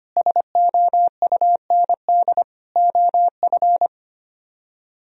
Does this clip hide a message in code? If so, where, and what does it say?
Yes: Morse code from 0.2 to 3.9 s, heard as a keyed tone, about 33 wpm, 700 Hz. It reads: SOUND OF